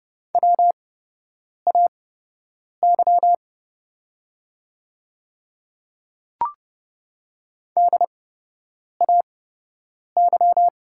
Morse code, heard as a keyed tone, about 30 wpm, spaced as for 8 wpm, 700 Hz. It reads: WAY DAY
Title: 8x Character Spacing - Top 100 Words - 30wpm